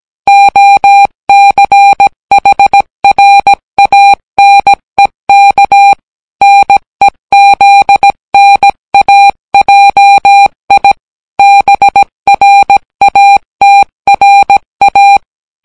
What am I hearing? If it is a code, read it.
Morse code, heard as a keyed tone, about 17 wpm, 800 Hz. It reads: OCHRANEK NEZNAJI BRATRA